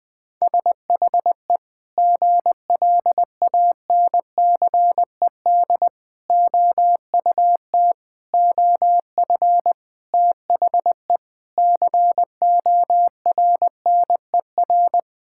Morse code, heard as a keyed tone, about 20 wpm, 700 Hz. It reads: SHE GLANCED OUT OF THE CORNER
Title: Sentences with 5-Word Phrases - Spoken Beginning - 20wpm